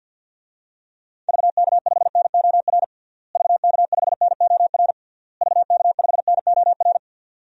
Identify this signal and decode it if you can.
Morse code, heard as a keyed tone, about 50 wpm, 700 Hz. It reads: VK5NOR VK5NOR VK5NOR
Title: DXCC-100 Call Signs - 6 Characters - Repeated 3x - 50wpm